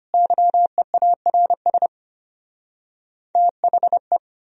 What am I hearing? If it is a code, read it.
Morse code, heard as a keyed tone, about 33 wpm, 700 Hz. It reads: YEARS THE